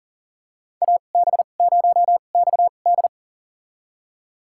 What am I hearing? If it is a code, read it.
Morse code, heard as a keyed tone, about 37 wpm, spaced as for 28 wpm, 700 Hz. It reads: AB0XD
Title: USA Call Signs - 5 Characters - 2x Character Spacing - 40wpm